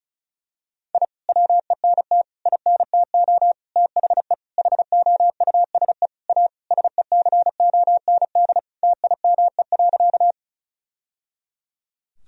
Morse code, heard as a keyed tone, about 36 wpm, 700 Hz. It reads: I WENT INTO THE HOUSE A SECOND TIME.
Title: Sentences from Top 1000 Words - Review Part 1 - 35wpm